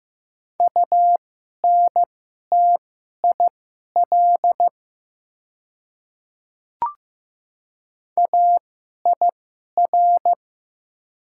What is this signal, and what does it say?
Morse code, heard as a keyed tone, about 15 wpm, spaced as for 11 wpm, 700 Hz. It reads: UNTIL AIR
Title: 2x Character Spacing - Top 300 Words - 15wpm